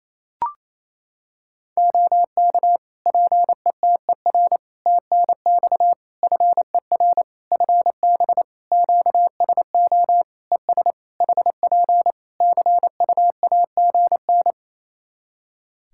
Morse code, heard as a keyed tone, about 27 wpm, 700 Hz. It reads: OK PETER TNX FER FB QSO ES HP CUAGN
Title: K1USN SST QSO Practice 28wpm